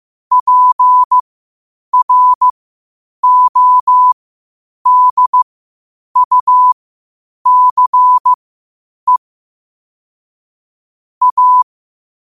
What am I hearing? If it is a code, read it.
Morse code, heard as a keyed tone, about 15 wpm, spaced as for 8 wpm, 1000 Hz. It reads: PRODUCE A
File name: 8wpm